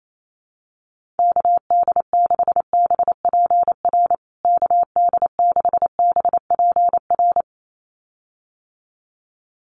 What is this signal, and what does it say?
Morse code, heard as a keyed tone, about 28 wpm, 700 Hz. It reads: KD6BPR KD6BPR
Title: USA Call Signs Repeated 2x 28wpm